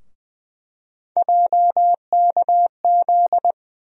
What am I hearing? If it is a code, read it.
Morse code, heard as a keyed tone, about 20 wpm, 700 Hz. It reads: JKZ
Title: USA Call Sign Postfixes - 3 Letters 20wpm